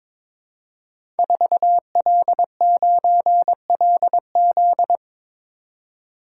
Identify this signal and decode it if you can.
Morse code, heard as a keyed tone, about 22 wpm, 700 Hz. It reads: 4L9LZ
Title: Code Groups - 5 Letter:Numbers - 22wpm